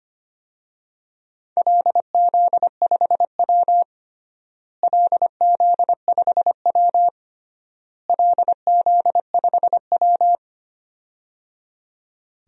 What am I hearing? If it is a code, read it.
Morse code, heard as a keyed tone, about 25 wpm, 700 Hz. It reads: LZ5W LZ5W LZ5W
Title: DXCC-100 Call Signs - 4 Characters - Repeated 3x - 25wpm